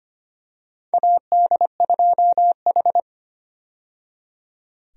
Morse code, heard as a keyed tone, about 25 wpm, 700 Hz. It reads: AD2H
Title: USA Call Signs - 4 Characters - 25wpm